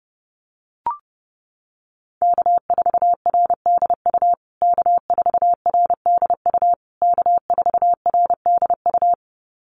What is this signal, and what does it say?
Morse code, heard as a keyed tone, about 28 wpm, 700 Hz. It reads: K4RDU K4RDU K4RDU
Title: USA Call Signs Repeated 3x 30wpm